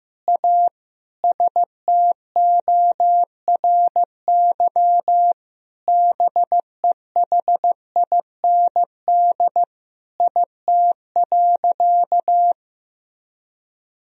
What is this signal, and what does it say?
Morse code, heard as a keyed tone, about 15 wpm, 700 Hz. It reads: A STORY BEHIND IT.
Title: Sentences from Top 500 Words and Abbreviations 15wpm